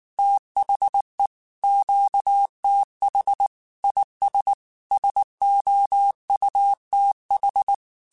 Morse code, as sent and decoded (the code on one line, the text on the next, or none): - .... / --.- - .... / .. ... / ... --- ..- - ....
THE QTH IS SOUTH